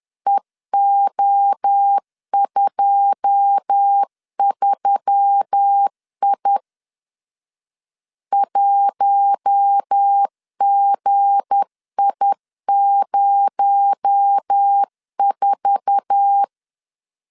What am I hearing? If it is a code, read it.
Morse code, heard as a keyed tone, about 10 wpm, 800 Hz. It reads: EO23I 1GI04
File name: LEZIONI / 9 2 10wpm 2736